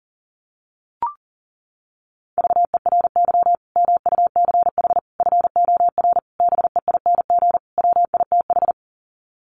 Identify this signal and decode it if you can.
Morse code, heard as a keyed tone, about 40 wpm, 700 Hz. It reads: VERY MUCH FOR BEING WITH
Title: Sets of 5 Words 40wpm